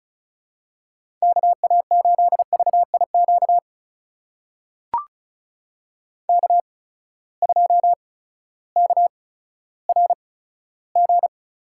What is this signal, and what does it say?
Morse code, heard as a keyed tone, about 36 wpm, spaced as for 16 wpm, 700 Hz. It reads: KA8VIQ K2KRG
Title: USA Call Signs - 8x Character Spacing - 35wpm